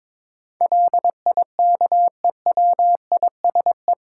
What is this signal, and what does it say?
Morse code, heard as a keyed tone, about 22 wpm, 700 Hz. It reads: LIKEWISE
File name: practice / 22wpm / Eight Letter Words - 22wpm